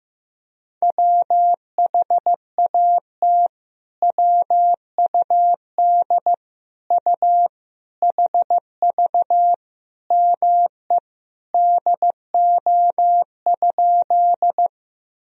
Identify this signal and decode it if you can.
Morse code, heard as a keyed tone, about 16 wpm, 700 Hz. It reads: WHAT WUD U HV ME DO?
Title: Sentences from Top 200 Words and Abbreviations 15wpm